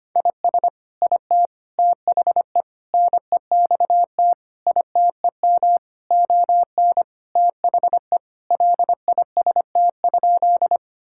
Morse code, heard as a keyed tone, about 25 wpm, 700 Hz. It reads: IS IT THE NEXT ITEM ON THE LIST?
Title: Sentences from Top 1000 Words - Review Part 6 - 25wpm